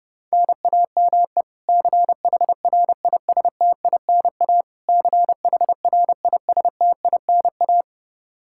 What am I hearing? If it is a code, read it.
Morse code, heard as a keyed tone, about 30 wpm, 700 Hz. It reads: NAME CHRISTINA CHRISTINA
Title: K1USN SST QSO Practice 30wpm